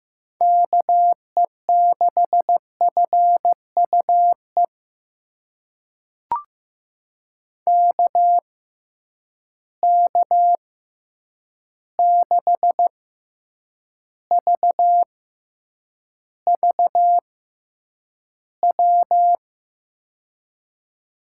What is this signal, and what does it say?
Morse code, heard as a keyed tone, about 15 wpm, spaced as for 8 wpm, 700 Hz. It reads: KE6FUE KK6VVW
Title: USA Call Signs - 6 Characters - 6x Character Spacing - 15wpm